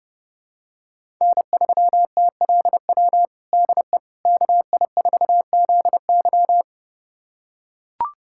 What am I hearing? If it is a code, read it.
Morse code, heard as a keyed tone, about 32 wpm, 700 Hz. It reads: N3TLW DE KI4ZY
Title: USA Callsign DE Callsign - 5 Characters - 30wpm